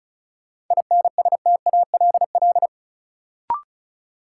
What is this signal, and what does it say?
Morse code, heard as a keyed tone, about 36 wpm, 700 Hz. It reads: INSTALL
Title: Sentences from Top 1600 Words - New Word Drill Alt - 35wpm